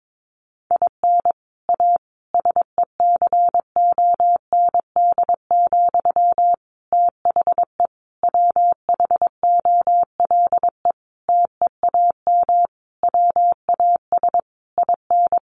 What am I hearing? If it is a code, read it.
Morse code, heard as a keyed tone, about 22 wpm, 700 Hz. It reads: IN A SECOND, THE WHOLE TEAM WAS IN
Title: Sentences from Top 1500 Words 22wpm